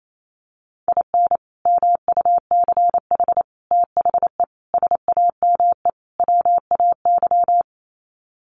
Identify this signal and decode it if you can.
Morse code, heard as a keyed tone, about 28 wpm, 700 Hz. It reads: IN MUCH THE SAME WAY
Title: Sets of 5 Words Encore 28wpm